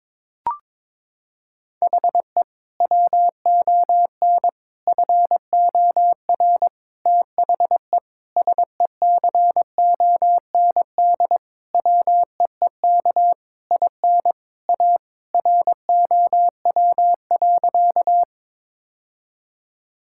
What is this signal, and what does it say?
Morse code, heard as a keyed tone, about 22 wpm, 700 Hz. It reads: HE WON FOR THE SECOND WEEK IN A ROW.
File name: Sentences with 7-Word Phrases - 22wpm